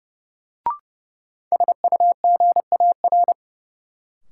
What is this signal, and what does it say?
Morse code, heard as a keyed tone, about 30 wpm, 700 Hz. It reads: SUGAR
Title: Sentences from Top 1200 Words - New Word Drill Alt - 30wpm